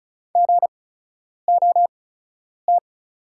Morse code, heard as a keyed tone, about 36 wpm, spaced as for 10 wpm, 700 Hz. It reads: GOT
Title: 8x Character Spacing - Top 300 Words - 35wpm